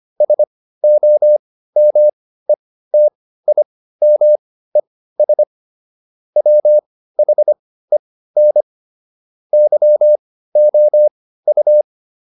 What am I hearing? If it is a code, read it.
Morse code, heard as a keyed tone, about 25 wpm, spaced as for 15 wpm, 600 Hz. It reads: SOMETIMES WHEN YOU